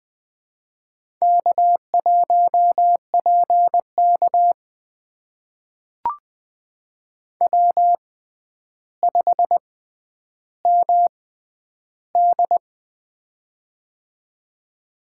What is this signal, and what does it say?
Morse code, heard as a keyed tone, about 20 wpm, spaced as for 10 wpm, 700 Hz. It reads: K1PK W5MD
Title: USA Call Signs - 4 Characters - 6x Character Spacing - 20wpm